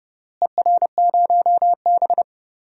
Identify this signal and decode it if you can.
Morse code, heard as a keyed tone, about 30 wpm, 700 Hz. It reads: ER0B
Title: DXCC-100 Call Signs - 4 Characters - Repeated 3x - 30wpm